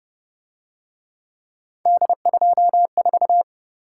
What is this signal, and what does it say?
Morse code, heard as a keyed tone, about 31 wpm, 700 Hz. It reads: D24
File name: Speed-Racing - Three Letter-Number Combinations - 25wpm